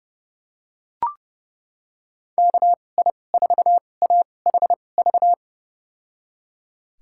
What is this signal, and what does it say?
Morse code, heard as a keyed tone, about 30 wpm, spaced as for 22 wpm, 700 Hz. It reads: KI4AHV
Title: USA Call Signs - 6 Characters - 2x Character Spacing - 30wpm